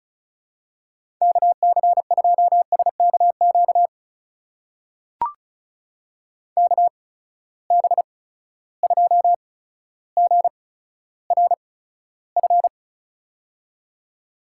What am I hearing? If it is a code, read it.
Morse code, heard as a keyed tone, about 34 wpm, spaced as for 15 wpm, 700 Hz. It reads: KC2SKQ KB2GRF
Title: USA Call Signs - 6 Characters - 8x Character Spacing - 35wpm